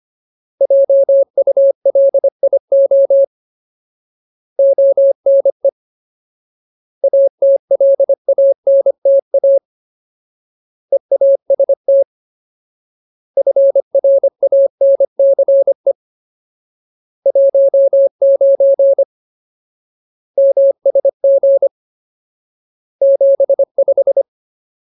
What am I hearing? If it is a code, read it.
Morse code, heard as a keyed tone, about 25 wpm, 550 Hz. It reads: JULIO ONE ATLANTA EAST FRANCE 19 MSG 75
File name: Ultimate Word List / 25WPM Ultimate